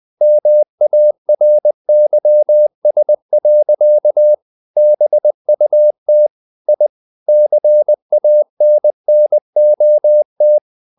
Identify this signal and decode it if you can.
Morse code, heard as a keyed tone, about 20 wpm, 600 Hz. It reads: MARYS. BUT I CANNOT
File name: the old man and the sea part 2 20wpm